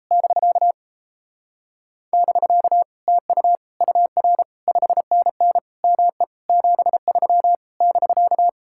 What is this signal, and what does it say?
Morse code, heard as a keyed tone, about 36 wpm, 700 Hz. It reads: <BK> <BK> TU UR 5NN ME 73 <BK>